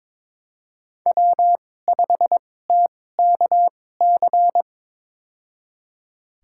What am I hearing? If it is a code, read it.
Morse code, heard as a keyed tone, about 22 wpm, spaced as for 16 wpm, 700 Hz. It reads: W5TKC
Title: USA Call Signs - 5 Characters - 2x Character Spacing - 22wpm